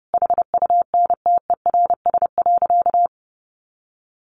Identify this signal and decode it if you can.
Morse code, heard as a keyed tone, about 30 wpm, 700 Hz. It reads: HUNTERS.
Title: A Sound Of Thunder 30wpm